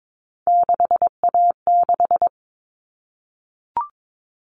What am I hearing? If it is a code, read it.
Morse code, heard as a keyed tone, about 22 wpm, 700 Hz. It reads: 6A6
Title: All Cut Numbers - 3-Digit 22wpm